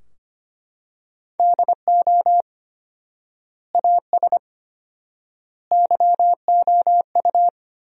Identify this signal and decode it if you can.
Morse code, heard as a keyed tone, about 25 wpm, 700 Hz. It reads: DO AS YOU
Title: 4x Word-Spacing - Sentences from Top 100 Words - 25wpm